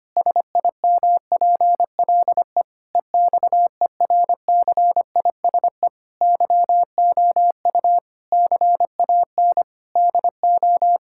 Morse code, heard as a keyed tone, about 25 wpm, 700 Hz. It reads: SIMPLE EXERCISE YOU CAN DO
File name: practice / 25wpm / Difficult Sentences with Hints - 25wpm